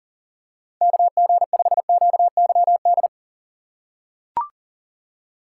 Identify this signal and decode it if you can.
Morse code, heard as a keyed tone, about 40 wpm, 700 Hz. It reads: KG5QYD
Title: USA Call Signs - 6 Characters - 2x Character Spacing - 40wpm